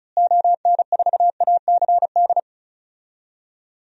Morse code, heard as a keyed tone, about 36 wpm, 700 Hz. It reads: ON4ACD
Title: DXCC-100 Call Signs - 6 Characters - Repeated 2x - 35wpm